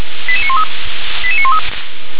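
Five-tone selective call (ZVEI), heard as quick runs of five tones, tones of about 70 ms. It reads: 88913 88913
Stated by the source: ZVEI five-tone selective call received on an SDR.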